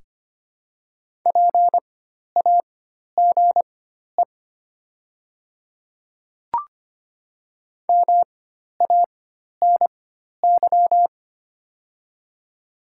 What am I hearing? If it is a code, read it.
Morse code, heard as a keyed tone, about 25 wpm, spaced as for 12 wpm, 700 Hz. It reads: PAGE MANY
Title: 4x Character Spacing - Top 300 Words - 25wpm